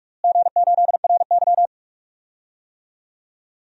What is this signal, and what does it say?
Morse code, heard as a keyed tone, about 46 wpm, 700 Hz. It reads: G8RY
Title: DXCC-100 Call Signs - 4 Characters - Repeated 2x - 45wpm